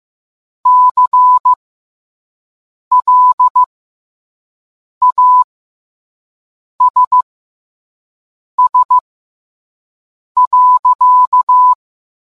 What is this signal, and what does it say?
Morse code, heard as a keyed tone, about 15 wpm, spaced as for 5 wpm, 1000 Hz. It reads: CLASS.